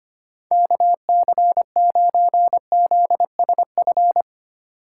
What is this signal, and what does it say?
Morse code, heard as a keyed tone, about 25 wpm, 700 Hz. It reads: KC9ZSF
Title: USA Call Signs - 3x Character Spacing - 25wpm